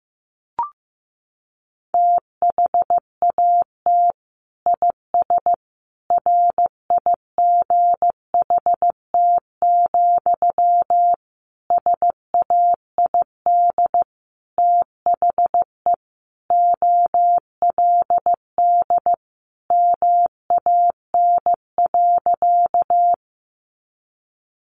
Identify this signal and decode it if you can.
Morse code, heard as a keyed tone, about 15 wpm, 700 Hz. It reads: THAT IS RIGHT, SAID THE OLD MAN.
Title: Sentences from Top 200 Words 15wpm